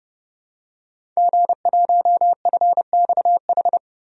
Morse code, heard as a keyed tone, about 30 wpm, 700 Hz. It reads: G1FXH
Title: DXCC-100 Call Signs - 5 Characters - 30wpm